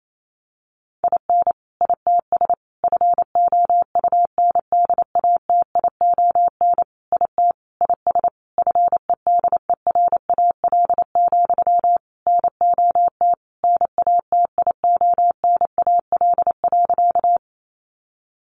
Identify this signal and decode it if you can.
Morse code, heard as a keyed tone, about 28 wpm, 700 Hz. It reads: IN ITS FOUNDATION IT IS FEDERAL, NOT NATIONAL.
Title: Sentences from Top 1200 Words 28wpm